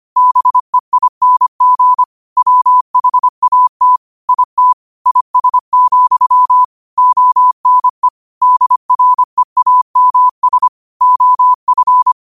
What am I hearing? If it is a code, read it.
Morse code, heard as a keyed tone, about 25 wpm, 1000 Hz. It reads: BEING WHAT IT IS, ONE DREAMS OF
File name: qotdfiles / 25wpm